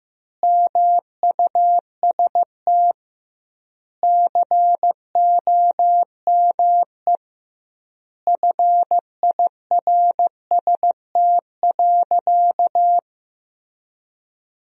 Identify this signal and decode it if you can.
Morse code, heard as a keyed tone, about 15 wpm, 700 Hz. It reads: MUST COME FIRST.
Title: 2x Word-Spacing - Sentences from Top 300 Words - 15wpm